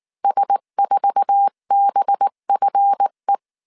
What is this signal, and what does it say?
Morse code, heard as a keyed tone, about 19 wpm, 800 Hz. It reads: S4BFE
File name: CW / DET / 1 9 1500C S C 18wpm 2539